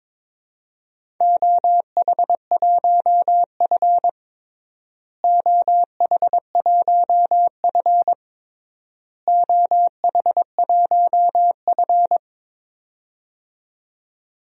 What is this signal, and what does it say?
Morse code, heard as a keyed tone, about 22 wpm, 700 Hz. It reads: OH1F OH1F OH1F
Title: DXCC-100 Call Signs - 4 Characters - Repeated 3x - 22wpm